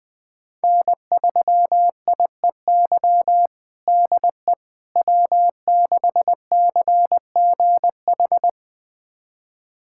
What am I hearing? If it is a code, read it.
Morse code, heard as a keyed tone, about 20 wpm, 700 Hz. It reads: N3IEY DE W6CGH
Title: USA Callsign DE Callsign - 5 Characters - 20wpm